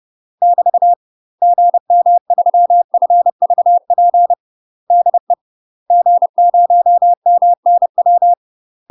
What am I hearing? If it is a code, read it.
Morse code, heard as a keyed tone, about 30 wpm, 700 Hz. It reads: <BT> GM3FVP DE G0MNW
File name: UK MASTER QSOs TRAINING / 30WPM / UK MASTER QSOs 0015